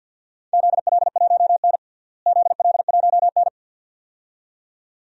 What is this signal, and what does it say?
Morse code, heard as a keyed tone, about 50 wpm, 700 Hz. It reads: ZL1N ZL1N
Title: DXCC-100 Call Signs - 4 Characters - Repeated 2x - 50wpm